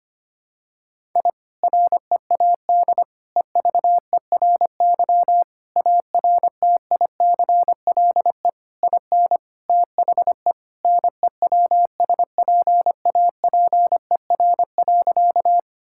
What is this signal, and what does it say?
Morse code, heard as a keyed tone, about 25 wpm, 700 Hz. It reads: I READ EVERY ARTICLE IN THE NEWSPAPER.
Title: Sentences from Top 1000 Words - Review Part 5 - 25wpm